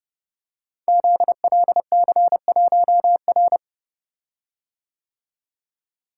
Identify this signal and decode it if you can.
Morse code, heard as a keyed tone, about 30 wpm, 700 Hz. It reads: ZLC1R
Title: Code Groups - 5 Letter:Numbers - 30wpm